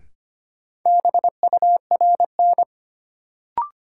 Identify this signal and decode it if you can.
Morse code, heard as a keyed tone, about 25 wpm, 700 Hz. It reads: BURN
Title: Sentences from Top 1300 Words - New Word Drill - 25wpm